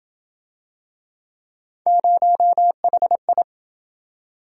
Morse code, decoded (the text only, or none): 0HI